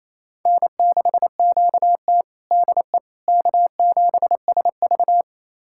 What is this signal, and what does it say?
Morse code, heard as a keyed tone, about 27 wpm, 700 Hz. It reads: N6QT DE K7SV
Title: USA Callsign DE Callsign - 4 Characters - 28wpm